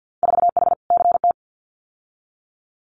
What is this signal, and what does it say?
Morse code, heard as a keyed tone, about 50 wpm, 700 Hz. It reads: 4H CT
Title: Winter Field Day Exchanges Encore 50wpm